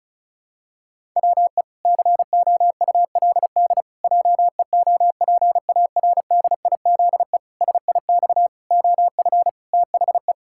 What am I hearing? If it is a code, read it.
Morse code, heard as a keyed tone, about 34 wpm, 700 Hz. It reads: WE COULD JEOPARDIZE SIX OF THE
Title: Pangram Challenge v2 - 35wpm